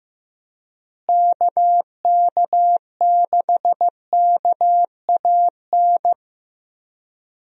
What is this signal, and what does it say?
Morse code, heard as a keyed tone, about 16 wpm, 700 Hz. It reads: KK6KAN
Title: USA Call Signs - 6 Characters - Repeated 2x - 15wpm